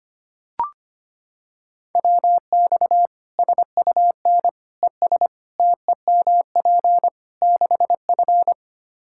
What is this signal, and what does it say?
Morse code, heard as a keyed tone, about 25 wpm, 700 Hz. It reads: WX SUN ES TEMP 6F
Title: Practice QSO - Exchange 2 - Info - 25wpm